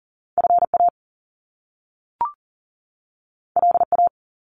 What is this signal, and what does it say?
Morse code, heard as a keyed tone, about 39 wpm, 700 Hz. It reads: FA LA
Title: Top Two Letter Combinations in Words 40wpm